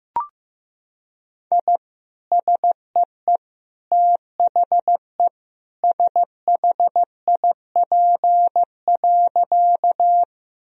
Morse code, heard as a keyed tone, about 15 wpm, 700 Hz. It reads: I SEE THE SHIP.
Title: Sentences from Top 1000 Words - Review Part 2 - 15wpm